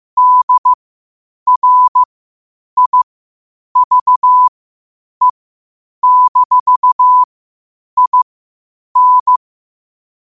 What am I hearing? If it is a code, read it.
Morse code, heard as a keyed tone, about 15 wpm, spaced as for 8 wpm, 1000 Hz. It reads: DRIVE-IN
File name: qotdfiles / 8wpm